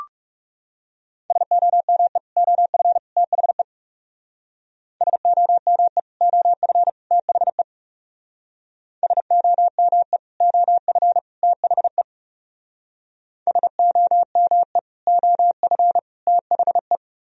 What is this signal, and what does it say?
Morse code, heard as a keyed tone, about 38 wpm, 700 Hz. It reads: SOME OF THE SOME OF THE SOME OF THE SOME OF THE